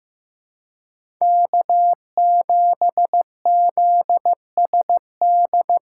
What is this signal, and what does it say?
Morse code, heard as a keyed tone, about 15 wpm, 700 Hz. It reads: K7ZSD
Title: USA Call Signs - 5 Characters - 4x Character Spacing - 15wpm